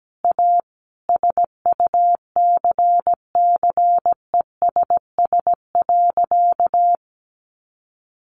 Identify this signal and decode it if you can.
Morse code, heard as a keyed tone, about 17 wpm, 700 Hz. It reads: A SUCCESS.
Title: Additional Easy Phrases 17wpm